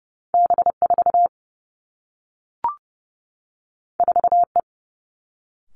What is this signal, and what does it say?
Morse code, heard as a keyed tone, about 30 wpm, 700 Hz. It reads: B 4 4 E
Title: Two Challenging Letter-Number Combinations 30wpm